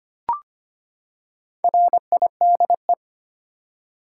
Morse code, heard as a keyed tone, about 25 wpm, 700 Hz. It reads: RIDE
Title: Four Letter Words - 25wpm